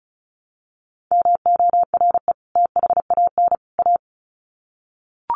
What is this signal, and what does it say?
Morse code, heard as a keyed tone, about 34 wpm, 700 Hz. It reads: MORE THAN A
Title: Sets of 3 Words 35wpm